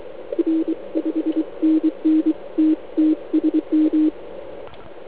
Morse code, heard as a keyed tone, about 23 wpm, 320 Hz. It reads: R 5NN TT SM
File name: Archiv / CUC149 / OK1IF CUC149